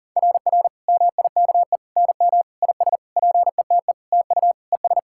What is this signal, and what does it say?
Morse code, heard as a keyed tone, about 40 wpm, 690 Hz. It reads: RR MIKE NM IS PETE TU ES
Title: SOTA QSO Practice 40wpm